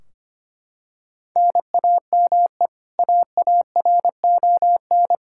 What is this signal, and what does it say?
Morse code, heard as a keyed tone, about 25 wpm, 700 Hz. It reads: NAME AARON
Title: Practice QSO - Exchange 1 - Beginning - 25wpm